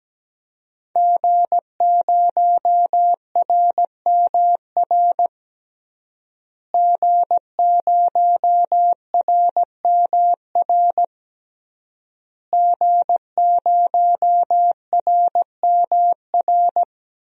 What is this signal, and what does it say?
Morse code, heard as a keyed tone, about 17 wpm, 700 Hz. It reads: G0RMR G0RMR G0RMR
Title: DXCC-100 Call Signs - 5 Characters - Repeated 3x - 17wpm